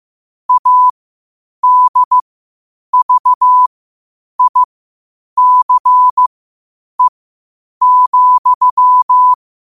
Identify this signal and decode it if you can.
Morse code, heard as a keyed tone, about 15 wpm, spaced as for 8 wpm, 1000 Hz. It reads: ADVICE,
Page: New quotes every day in morse code at 8 Words per minute.